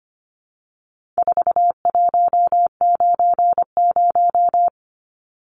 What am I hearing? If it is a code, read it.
Morse code, heard as a keyed tone, about 25 wpm, 700 Hz. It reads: 4190
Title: Numbers - 4-Digit 25wpm